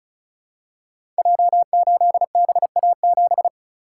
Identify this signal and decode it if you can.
Morse code, heard as a keyed tone, about 36 wpm, 700 Hz. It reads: J8BA7
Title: Code Groups - 5 Letter:Numbers - 35wpm